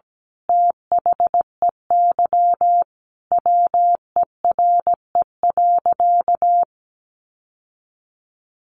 Morse code, heard as a keyed tone, about 17 wpm, 700 Hz. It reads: THEY WERE.
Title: Sentences from Top 1000 Words 17wpm